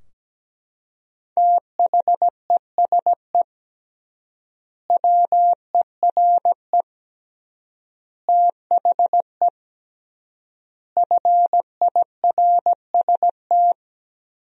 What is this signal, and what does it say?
Morse code, heard as a keyed tone, about 17 wpm, 700 Hz. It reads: THESE WERE THE FIRST
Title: 3x Word-Spacing - Sentences from Top 300 Words - 17wpm